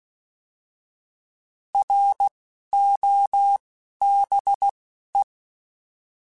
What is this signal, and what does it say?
Morse code, heard as a keyed tone, about 16 wpm, 800 Hz. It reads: R O B E